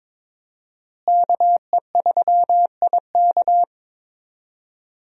Morse code, heard as a keyed tone, about 22 wpm, 700 Hz. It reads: KE3IK
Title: USA Call Signs - 5 Characters - Repeated 3x - 22wpm